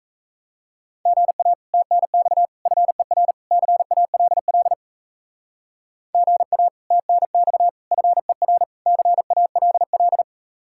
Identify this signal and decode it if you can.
Morse code, heard as a keyed tone, about 37 wpm, 700 Hz. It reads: GA TNX FER CALL GA TNX FER CALL